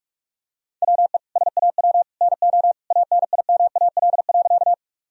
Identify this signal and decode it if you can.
Morse code, heard as a keyed tone, about 43 wpm, 700 Hz. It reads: WE SAW NO ANIMAL.
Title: Sentences from Top 1000 Words - Review Part 1 - 45wpm